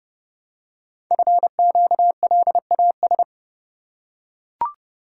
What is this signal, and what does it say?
Morse code, heard as a keyed tone, about 30 wpm, 700 Hz. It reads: FQLAS
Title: Code Groups - 5 Letters - 30wpm